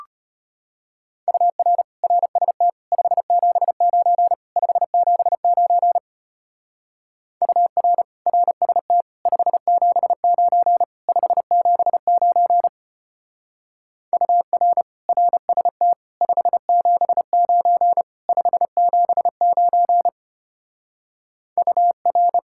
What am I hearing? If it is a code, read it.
Morse code, heard as a keyed tone, about 31 wpm, 700 Hz. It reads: UR RST 579 579 UR RST 579 579 UR RST 579 579 UR
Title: Speed-Racing Common QSO Phrases Encore - 25wpm